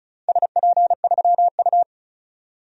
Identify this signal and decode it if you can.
Morse code, heard as a keyed tone, about 34 wpm, 700 Hz. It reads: SP3U
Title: DXCC-100 Call Signs - 4 Characters - 35wpm